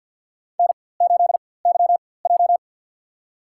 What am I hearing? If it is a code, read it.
Morse code, heard as a keyed tone, about 51 wpm, spaced as for 23 wpm, 700 Hz. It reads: N8YJ
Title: USA Call Signs - 4x Character Spacing - 50wpm